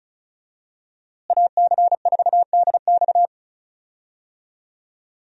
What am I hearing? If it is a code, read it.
Morse code, heard as a keyed tone, about 34 wpm, 700 Hz. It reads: AC4DX
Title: USA Call Signs - 5 Characters - Repeated 3x - 35wpm